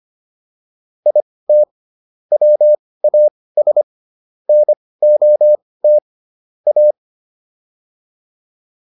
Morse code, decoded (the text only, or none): IT WAS NOT A